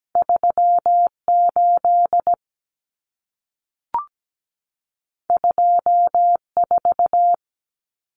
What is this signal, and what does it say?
Morse code, heard as a keyed tone, about 17 wpm, 700 Hz. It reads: 38 24
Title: Numbers - 2-Digit 17wpm